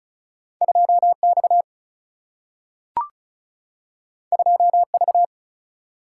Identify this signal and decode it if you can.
Morse code, heard as a keyed tone, about 36 wpm, 700 Hz. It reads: 2X 2V
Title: Single-Letter Single-Number 35wpm